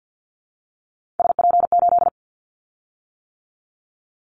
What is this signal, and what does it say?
Morse code, heard as a keyed tone, about 50 wpm, 700 Hz. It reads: SP8
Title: Callsign Prefixes for Top 25 DXCC - 50wpm